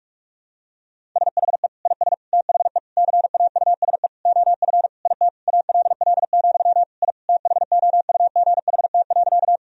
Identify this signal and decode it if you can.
Morse code, heard as a keyed tone, about 46 wpm, 700 Hz. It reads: SHE IS THE CAUSE OF IT ALL, I THOUGHT.
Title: Sentences from Top 1000 Words - Review Part 1 - 45wpm